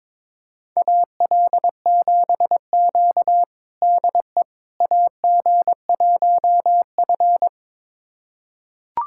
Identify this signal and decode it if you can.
Morse code, heard as a keyed tone, about 22 wpm, 700 Hz. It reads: AL7Q DE AG1F
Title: USA Callsign DE Callsign - 4 Characters - 22wpm